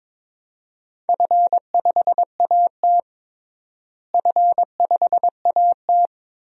Morse code, heard as a keyed tone, about 21 wpm, 700 Hz. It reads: F5AT F5AT
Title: DXCC-100 Call Signs - 4 Characters - Repeated 2x - 22wpm